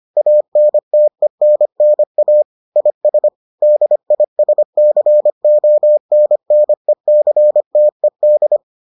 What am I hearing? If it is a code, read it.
Morse code, heard as a keyed tone, about 25 wpm, 600 Hz. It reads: ANTENNA IS DISCONNECTED